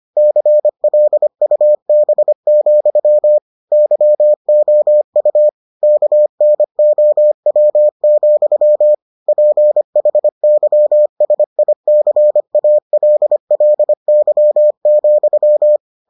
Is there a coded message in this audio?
Yes: Morse code from 0.2 to 15.8 s, heard as a keyed tone, about 25 wpm, 600 Hz. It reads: CLUB, YOU KNOW, PHYSICALLY,